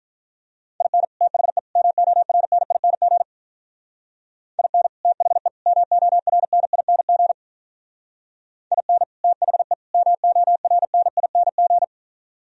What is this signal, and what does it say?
Morse code, heard as a keyed tone, about 45 wpm, 700 Hz. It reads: IN THE MORNING IN THE MORNING IN THE MORNING